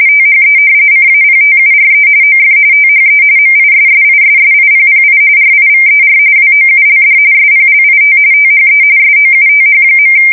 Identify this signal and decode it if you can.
RTTY, about 45.45 baud. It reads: THE CAPTURE OF THE MISSISSIPPI RIVER WOULD SPLIT THE SOUTH.
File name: fsk170 / usnavy3